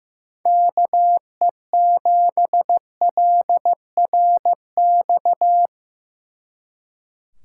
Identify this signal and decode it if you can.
Morse code, heard as a keyed tone, about 18 wpm, 700 Hz. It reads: KE7LRX
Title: Speed Racing - USA 2x3 Call Signs 15wpm